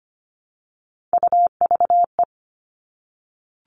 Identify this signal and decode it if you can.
Morse code, heard as a keyed tone, about 25 wpm, 700 Hz. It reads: UVE
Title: All Cut Numbers - 3-Digit 25wpm